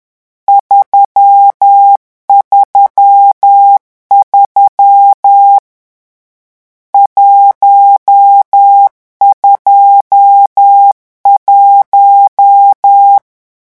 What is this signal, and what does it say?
Morse code, heard as a keyed tone, about 10 wpm, 800 Hz. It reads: 333 121
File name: morse mp3 / lecon10-vitesse 10